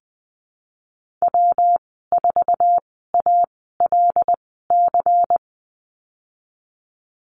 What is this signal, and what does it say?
Morse code, heard as a keyed tone, about 19 wpm, spaced as for 14 wpm, 700 Hz. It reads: W4ALC
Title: 179 Lesson 29g - Slash - Call Signs 20wpm